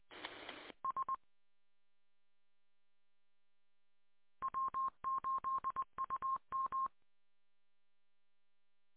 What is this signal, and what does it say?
Morse code, heard as a keyed tone, about 20 wpm, 1060 Hz. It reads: S W8UM